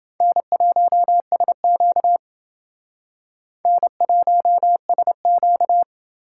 Morse code, heard as a keyed tone, about 29 wpm, 700 Hz. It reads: N1SQ N1SQ